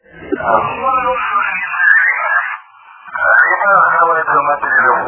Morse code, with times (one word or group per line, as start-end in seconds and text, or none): none